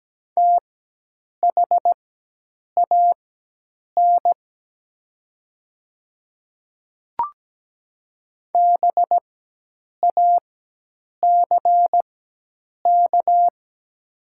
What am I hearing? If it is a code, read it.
Morse code, heard as a keyed tone, about 17 wpm, spaced as for 8 wpm, 700 Hz. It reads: THAN BACK
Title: 4x Character Spacing - Top 200 Words - 17wpm